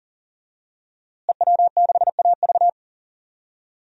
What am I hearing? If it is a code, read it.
Morse code, heard as a keyed tone, about 40 wpm, 700 Hz. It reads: EW6AV
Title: DXCC-100 Call Signs - 5 Characters - Repeated 2x - 40wpm